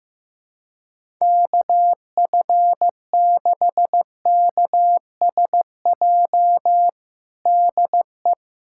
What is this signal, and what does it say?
Morse code, heard as a keyed tone, about 15 wpm, 700 Hz. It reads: KF6KSJ DE
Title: USA Callsign DE Callsign 15wpm